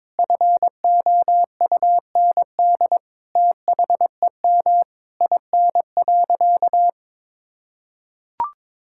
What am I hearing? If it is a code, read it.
Morse code, heard as a keyed tone, about 22 wpm, 700 Hz. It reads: FOUND THEM IN.
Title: Sentences from Top 1000 Words - Review Part 6 - 22wpm